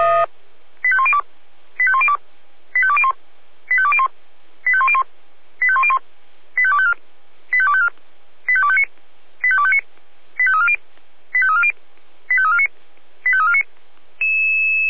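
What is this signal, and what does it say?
Five-tone selective call (ZVEI), heard as quick runs of five tones, tones of about 70 ms. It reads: 85102 85102 85201 85201 85101 85101 85254 85254 85259 85259 85340 85340 85349 85349
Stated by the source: ZVEI five-tone selective call received with an SDR.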